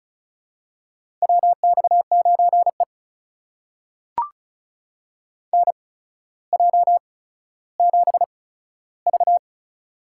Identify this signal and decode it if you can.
Morse code, heard as a keyed tone, about 36 wpm, spaced as for 15 wpm, 700 Hz. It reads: WX9E NJ7V
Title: USA Call Signs - 4 Characters - 8x Character Spacing - 35wpm